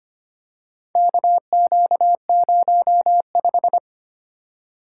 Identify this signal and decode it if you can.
Morse code, heard as a keyed tone, about 25 wpm, 700 Hz. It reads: KQ05
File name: practice / 25wpm / Four Letter-Number Combinations - 25wpm